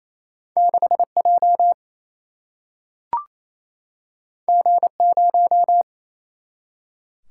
Morse code, heard as a keyed tone, about 28 wpm, 700 Hz. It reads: 6J G0
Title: Single-Letter Single-Number 28wpm